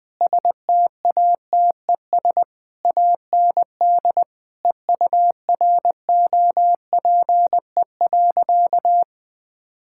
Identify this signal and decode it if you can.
Morse code, heard as a keyed tone, about 20 wpm, 700 Hz. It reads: STATES AND EUROPE.
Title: Sentences with 5-Word Phrases - 20wpm